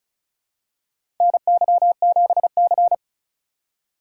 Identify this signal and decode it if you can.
Morse code, heard as a keyed tone, about 36 wpm, 700 Hz. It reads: NY7C
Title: USA Call Signs - 4 Characters - 4x Character Spacing - 35wpm